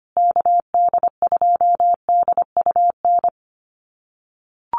Morse code, heard as a keyed tone, about 25 wpm, 700 Hz. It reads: KD2DUN
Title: USA Call Signs Repeated 2x 25wpm